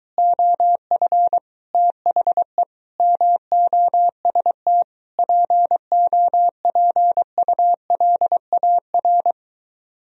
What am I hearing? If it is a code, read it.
Morse code, heard as a keyed tone, about 23 wpm, 700 Hz. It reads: OF THE MOST POPULAR